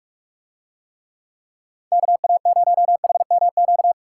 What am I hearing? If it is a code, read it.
Morse code, heard as a keyed tone, about 44 wpm, 700 Hz. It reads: KA0HMQ